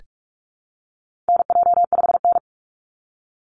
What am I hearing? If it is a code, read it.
Morse code, heard as a keyed tone, about 44 wpm, 700 Hz. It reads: NJ5N
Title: Call Signs Repeated 3x - 45wpm